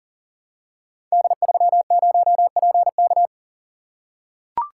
Morse code, heard as a keyed tone, about 40 wpm, 700 Hz. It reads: D30PK
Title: Code Groups - 5 Letter:Numbers - 40wpm